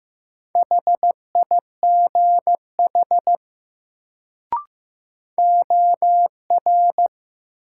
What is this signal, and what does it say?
Morse code, heard as a keyed tone, about 15 wpm, 700 Hz. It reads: HIGH OR
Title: Sets of 2 Words - New Word Drill - 15wpm